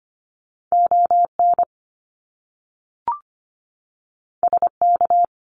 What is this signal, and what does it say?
Morse code, heard as a keyed tone, about 25 wpm, 700 Hz. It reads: ON SK
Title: Canadian Provinces and Territories 25wpm